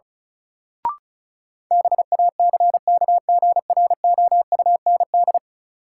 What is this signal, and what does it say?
Morse code, heard as a keyed tone, about 34 wpm, 700 Hz. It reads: BACKGROUND
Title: Sentences from Top 1300 Words - New Word Drill Alt - 35wpm